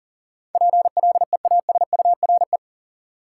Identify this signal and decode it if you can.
Morse code, heard as a keyed tone, about 40 wpm, 700 Hz. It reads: PLEASURE
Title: Sentences from Top 1600 Words - New Word Drill Alt - 40wpm